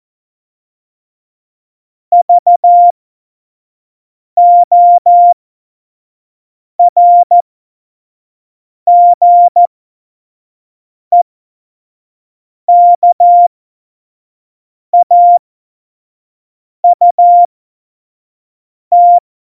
Morse code, heard as a keyed tone, about 14 wpm, spaced as for 5 wpm, 700 Hz. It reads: VORGEKAUT
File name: Morsekurs / Witze / Witz 11. September 2018, 25 F70